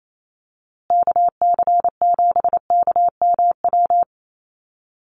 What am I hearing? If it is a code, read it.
Morse code, heard as a keyed tone, about 28 wpm, 700 Hz. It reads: KC7KMW
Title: USA Call Signs Repeated 3x 28wpm